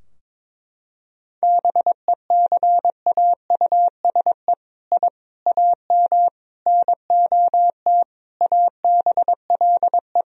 Morse code, heard as a keyed tone, about 22 wpm, 700 Hz. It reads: BECAUSE I AM NOT ABLE
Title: Sentences from Top 1000 Words - Review Part 3 - 22wpm